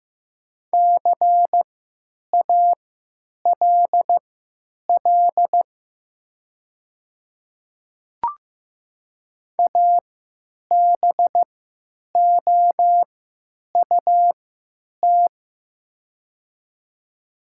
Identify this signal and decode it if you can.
Morse code, heard as a keyed tone, about 15 wpm, spaced as for 8 wpm, 700 Hz. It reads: CALL ABOUT
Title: 3x Character Spacing - Top 500 Words - 15wpm